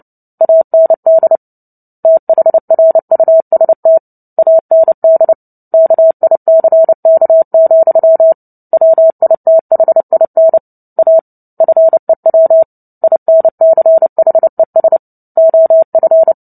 Morse code, heard as a keyed tone, about 30 wpm, 650 Hz. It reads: AND THRUST AND KICK, WITHIN A FEW INCHES OF
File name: Book Two / 03-Chapter 3 - 29 WPM